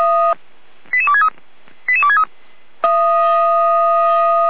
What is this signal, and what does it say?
Five-tone selective call (ZVEI), heard as quick runs of five tones, tones of about 70 ms. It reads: 88263 88263